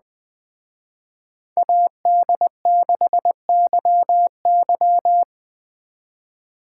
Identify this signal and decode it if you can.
Morse code, heard as a keyed tone, about 19 wpm, 700 Hz. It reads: AD6YY
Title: Speed Racing - USA 2x2 Call Signs 17wpm